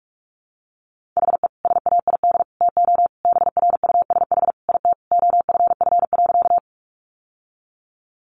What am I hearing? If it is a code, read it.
Morse code, heard as a keyed tone, about 44 wpm, 700 Hz. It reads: HE SAID TO BRUSH IT OFF.
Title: Sentences from Top 1100 Words 45wpm